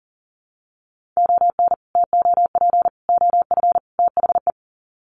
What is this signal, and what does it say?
Morse code, heard as a keyed tone, about 40 wpm, 700 Hz. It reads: ON TOP OF THE
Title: Sets of 4 Words 40wpm